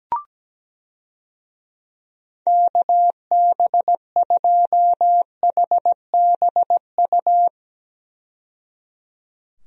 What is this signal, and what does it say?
Morse code, heard as a keyed tone, about 17 wpm, 700 Hz. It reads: KB2HBU
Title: USA Call Signs - 6 Characters - 17wpm